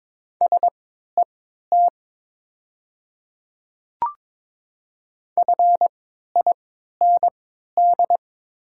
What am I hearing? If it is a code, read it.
Morse code, heard as a keyed tone, about 22 wpm, spaced as for 12 wpm, 700 Hz. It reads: SET FIND
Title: 3x Character Spacing - Top 200 Words - 22wpm